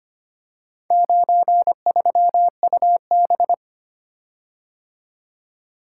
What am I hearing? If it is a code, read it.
Morse code, heard as a keyed tone, about 25 wpm, 700 Hz. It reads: 93UB
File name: practice / 25wpm / Four Letter-Number Combinations - 25wpm